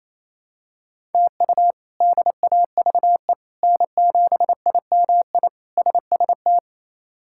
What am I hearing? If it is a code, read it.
Morse code, heard as a keyed tone, about 27 wpm, 700 Hz. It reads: TU DAVE N7IMI SST
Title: K1USN SST QSO Practice - 28wpm